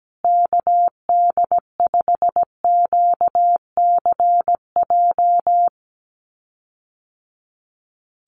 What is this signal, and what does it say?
Morse code, heard as a keyed tone, about 17 wpm, 700 Hz. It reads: KD5QCJ
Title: USA Call Signs Repeated 3x 17wpm